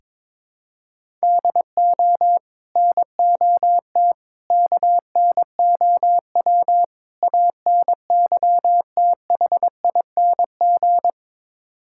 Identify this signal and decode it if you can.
Morse code, heard as a keyed tone, about 22 wpm, 700 Hz. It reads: DO NOT KNOW ANYTHING
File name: practice / 22wpm / Build A Phrase - 5-Word Phrases - 22wpm